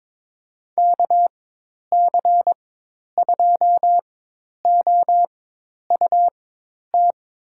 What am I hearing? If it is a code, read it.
Morse code, heard as a keyed tone, about 21 wpm, spaced as for 10 wpm, 700 Hz. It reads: KC2OUT
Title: USA Call Signs - 6 Characters - 4x Character Spacing - 22wpm